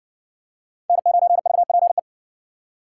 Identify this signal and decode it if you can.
Morse code, heard as a keyed tone, about 58 wpm, 700 Hz. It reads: N9VPE